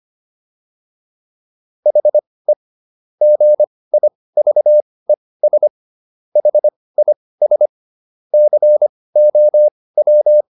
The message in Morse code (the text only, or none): HE GIVES HIS COW